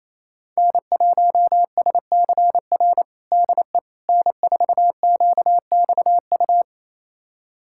Morse code, heard as a keyed tone, about 28 wpm, 700 Hz. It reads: N1SCR DE N4QXU
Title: USA Callsign DE Callsign - 5 Characters - 28wpm